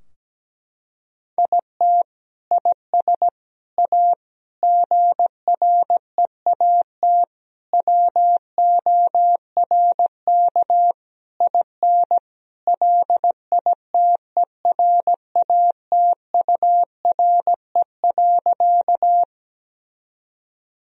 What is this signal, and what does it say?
Morse code, heard as a keyed tone, about 17 wpm, 700 Hz. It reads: IT IS A GREAT WORK IN LITERATURE.
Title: Sentences from Top 1000 Words - Review Part 3 - 17wpm